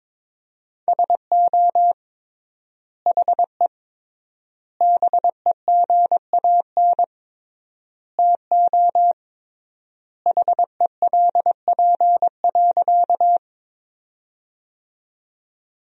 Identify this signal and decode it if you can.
Morse code, heard as a keyed tone, about 22 wpm, 700 Hz. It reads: SO HE BEGAN TO HELP.
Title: 3x Word-Spacing - Sentences from Top 300 Words - 22wpm